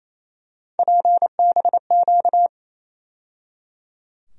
Morse code, heard as a keyed tone, about 28 wpm, 700 Hz. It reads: PBQ
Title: USA Call Sign Postfixes - 3 Letters 28wpm